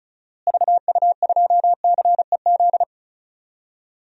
Morse code, heard as a keyed tone, about 36 wpm, 700 Hz. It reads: VU2CEZ
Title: DXCC-100 Call Signs - 6 Characters - Repeated 2x - 35wpm